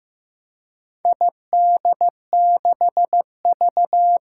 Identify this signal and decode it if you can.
Morse code, heard as a keyed tone, about 15 wpm, 700 Hz. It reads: ID6V
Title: Four Challenging Letter-Number Combinations 15wpm